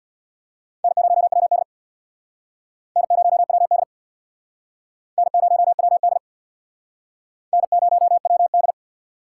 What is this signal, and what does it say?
Morse code, heard as a keyed tone, about 66 wpm, 700 Hz. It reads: N0WD N0WD N0WD N0WD